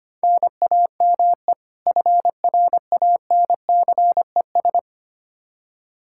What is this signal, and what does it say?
Morse code, heard as a keyed tone, about 25 wpm, 700 Hz. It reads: NAME FRANCES
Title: Practice QSO - Exchange 1 - Beginning - 25wpm